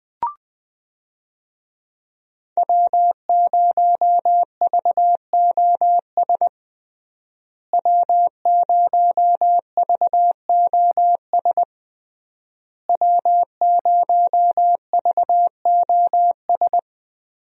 Morse code, heard as a keyed tone, about 20 wpm, 700 Hz. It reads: W0VOS W0VOS W0VOS
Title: USA Call Signs - 5 Characters - Repeated 3x - 20wpm